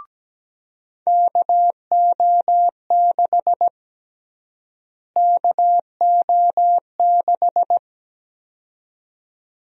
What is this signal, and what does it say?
Morse code, heard as a keyed tone, about 17 wpm, 700 Hz. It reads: KO6 KO6
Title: USA Call Sign Prefixes - Repeated 2x - 17wpm